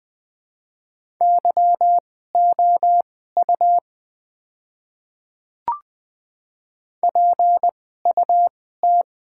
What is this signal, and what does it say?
Morse code, heard as a keyed tone, about 20 wpm, spaced as for 14 wpm, 700 Hz. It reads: YOU PUT
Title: 2x Character Spacing - Top 200 Words - 20wpm